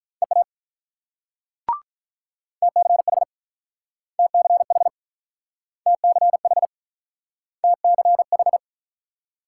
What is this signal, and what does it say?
Morse code, heard as a keyed tone, about 44 wpm, 700 Hz. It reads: EA TCH TCH TCH TCH